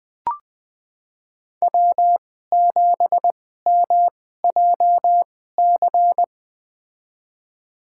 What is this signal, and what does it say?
Morse code, heard as a keyed tone, about 20 wpm, spaced as for 15 wpm, 700 Hz. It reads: W7MJC
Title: USA Call Signs - 2x Character Spacing - 20wpm